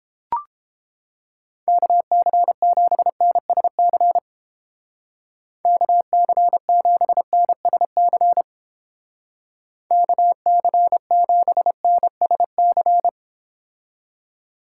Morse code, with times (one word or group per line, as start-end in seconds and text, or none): 1.7-4.2 KC7NSC
5.7-8.4 KC7NSC
9.9-13.1 KC7NSC